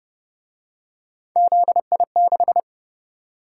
Morse code, heard as a keyed tone, about 28 wpm, 700 Hz. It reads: ZI6